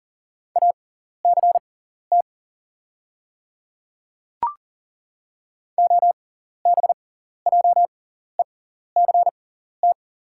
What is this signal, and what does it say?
Morse code, heard as a keyed tone, about 41 wpm, spaced as for 14 wpm, 700 Hz. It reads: ACT OBJECT
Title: 6x Character Spacing - Top 500 Words - 40wpm